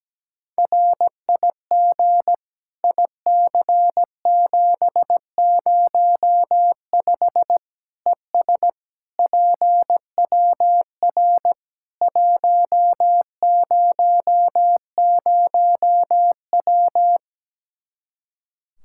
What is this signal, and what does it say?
Morse code, heard as a keyed tone, about 17 wpm, 700 Hz. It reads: RIG IC705 ES PWR 100W
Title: Practice QSO - Exchange 2 - Info - 17wpm